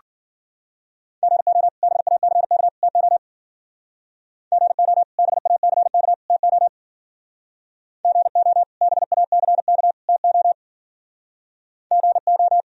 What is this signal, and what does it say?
Morse code, heard as a keyed tone, about 50 wpm, 700 Hz. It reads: GO BACK TO GO BACK TO GO BACK TO GO
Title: Speed-Racing - Sets of 3 Words 40wpm